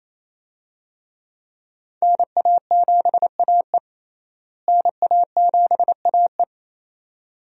Morse code, heard as a keyed tone, about 28 wpm, 700 Hz. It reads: NA7AE NA7AE
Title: USA Call Signs - 5 Characters - Repeated 2x - 28wpm